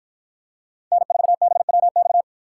Morse code, heard as a keyed tone, about 53 wpm, 700 Hz. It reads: N4BWX